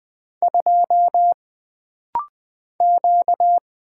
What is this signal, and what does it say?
Morse code, heard as a keyed tone, about 20 wpm, 700 Hz. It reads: 2 Q
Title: Single Character (Letter:Number:Punctuation:Prosign) - Rapid-Fire - 20wpm